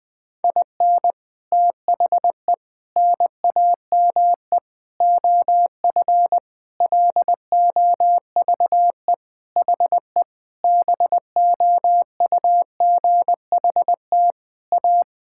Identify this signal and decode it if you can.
Morse code, heard as a keyed tone, about 20 wpm, 700 Hz. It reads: IN THE NAME OF LOVE HE BOUGHT A
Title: Sentences with 4-Word Phrases - Spoken Beginning - 20wpm